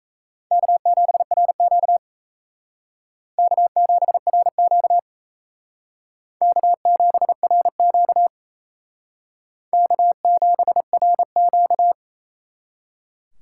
Morse code, heard as a keyed tone, about 35 wpm, 700 Hz. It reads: K7RQ K7RQ K7RQ K7RQ
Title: Speed Racing - USA 1x2 Call Signs 28wpm